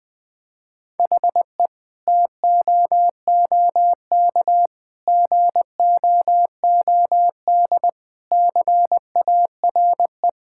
Morse code, heard as a keyed tone, about 20 wpm, 700 Hz. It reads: HE TOOK GOOD CARE
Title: Sentences from Top 1000 Words - Review Part 5 - 20wpm